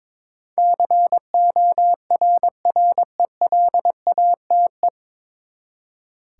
Morse code, heard as a keyed tone, about 22 wpm, 700 Hz. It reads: CORRELATE
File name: practice / 22wpm / Nine Letter Words - 22wpm